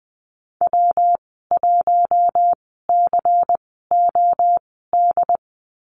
Morse code, heard as a keyed tone, about 20 wpm, spaced as for 14 wpm, 700 Hz. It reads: W1COD
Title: 128 Lesson 21g - Number 6 - Call Signs 20wpm